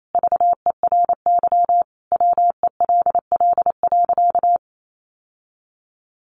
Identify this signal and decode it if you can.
Morse code, heard as a keyed tone, about 28 wpm, 700 Hz. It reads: VERY WELL.
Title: Sentences from Top 1400 Words 28wpm